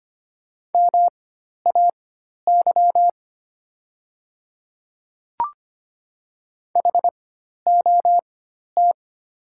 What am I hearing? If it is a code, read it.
Morse code, heard as a keyed tone, about 25 wpm, spaced as for 12 wpm, 690 Hz. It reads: MAY HOT
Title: 4x Character Spacing - Top 100 Words - 25wpm